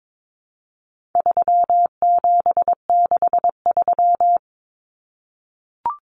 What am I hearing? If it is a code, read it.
Morse code, heard as a keyed tone, about 22 wpm, 700 Hz. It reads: 3763
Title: Common Cut Numbers - 4-Digit 22wpm